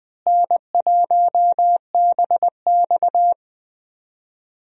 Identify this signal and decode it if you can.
Morse code, heard as a keyed tone, about 19 wpm, 700 Hz. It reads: N1BX
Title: USA Call Signs - 4 Characters - Repeated 3x - 20wpm